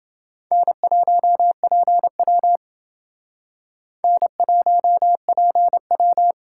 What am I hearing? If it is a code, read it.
Morse code, heard as a keyed tone, about 29 wpm, 700 Hz. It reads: N1PW N1PW